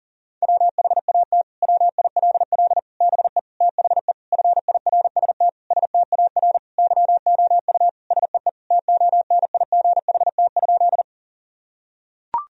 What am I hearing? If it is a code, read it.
Morse code, heard as a keyed tone, about 40 wpm, 700 Hz. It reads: WHAT WILL BE THE FIRST STAR YOU SEE TONIGHT?
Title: Sentences from Top 1000 Words - Review Part 2 - 40wpm